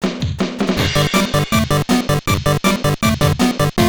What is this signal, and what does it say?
There is no code.